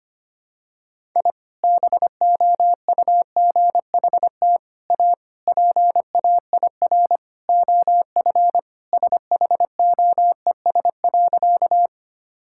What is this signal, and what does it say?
Morse code, heard as a keyed tone, about 25 wpm, 700 Hz. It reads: I BOUGHT A PAIR OF SHOES.
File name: practice / 25wpm / Sentences with 3-Word Phrases - 25wpm